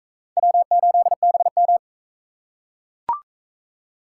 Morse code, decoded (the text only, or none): W8BM